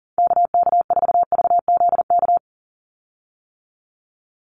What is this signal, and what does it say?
Morse code, heard as a keyed tone, about 40 wpm, 700 Hz. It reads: KK4VZK
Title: USA Call Signs 40wpm